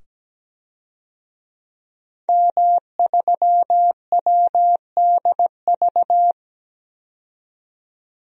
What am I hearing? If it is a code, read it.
Morse code, heard as a keyed tone, about 17 wpm, 700 Hz. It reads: M3WDV
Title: DXCC-100 Call Signs - 5 Characters - 17wpm